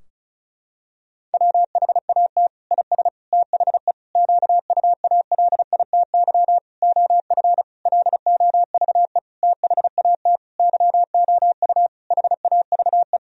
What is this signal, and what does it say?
Morse code, heard as a keyed tone, about 36 wpm, 700 Hz. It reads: WHAT IS THE QUALITY OF LOVE THAT YOU HAVE
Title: Sentences from Top 1000 Words - Review Part 4 - 35wpm